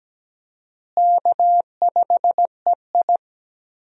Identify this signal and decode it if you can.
Morse code, heard as a keyed tone, about 17 wpm, 700 Hz. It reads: K5EI
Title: USA Call Signs - 4 Characters - Repeated 3x - 17wpm